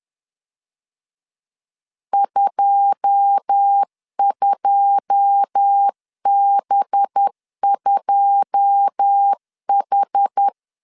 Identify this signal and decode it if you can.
Morse code, heard as a keyed tone, about 10 wpm, 800 Hz. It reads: 22B2H
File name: LEZIONI / 9 2 10wpm 2736